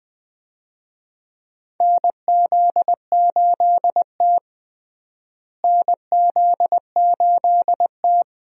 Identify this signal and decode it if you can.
Morse code, heard as a keyed tone, about 20 wpm, 700 Hz. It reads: NZ8T NZ8T
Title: USA Call Signs - 4 Characters - Repeated 3x - 20wpm